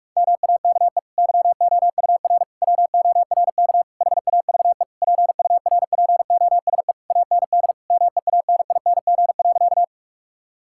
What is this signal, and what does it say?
Morse code, heard as a keyed tone, about 46 wpm, 700 Hz. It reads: MAKE YOUR WORK HAVE PURPOSE AND MEANING.
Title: Sentences from Top 1000 Words - Review Part 6 - 45wpm